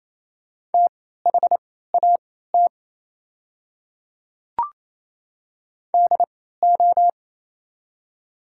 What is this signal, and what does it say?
Morse code, heard as a keyed tone, about 28 wpm, spaced as for 16 wpm, 700 Hz. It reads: THAT DO
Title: 3x Character Spacing - Top 100 Words - 28wpm